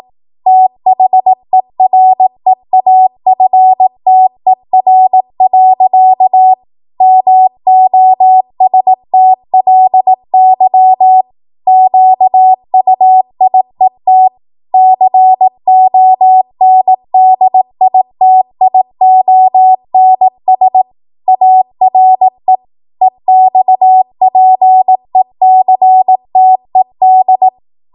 Morse code, heard as a keyed tone, about 18 wpm, 750 Hz. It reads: THEREAFTER. MOSTLY QUIET CONDITIONS ARE EXPECTED
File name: Morse / 18WPM